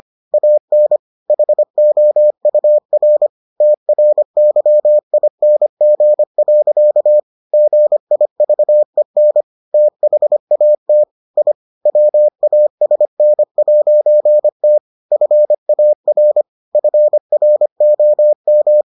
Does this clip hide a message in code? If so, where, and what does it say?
Yes: Morse code from 0.3 to 18.8 s, heard as a keyed tone, about 25 wpm, 600 Hz. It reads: AN HOUR TRYING. GIVEN THAT I WASN'T FAR FROM